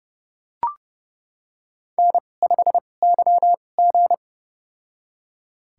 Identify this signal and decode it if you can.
Morse code, heard as a keyed tone, about 31 wpm, spaced as for 22 wpm, 700 Hz. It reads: N5YG
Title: USA Call Signs - 4 Characters - 2x Character Spacing - 30wpm